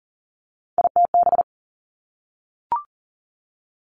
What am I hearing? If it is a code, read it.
Morse code, heard as a keyed tone, about 40 wpm, 700 Hz. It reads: ITB
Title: Three Challenging Letter-Number Combinations 40wpm